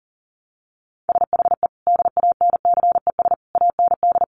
Morse code, heard as a keyed tone, about 40 wpm, 700 Hz. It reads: SHE DANCES AND
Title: Easy Phrases 40wpm